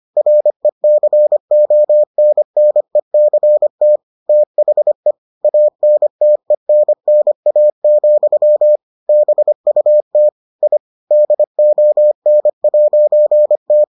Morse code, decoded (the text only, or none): RECONNECT THE ANTENNA, BUT I DON'T